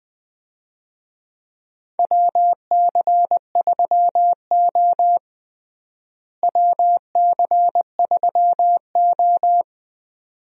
Morse code, heard as a keyed tone, about 20 wpm, 700 Hz. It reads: WC3O WC3O
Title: USA Call Signs - 4 Characters - Repeated 2x - 20wpm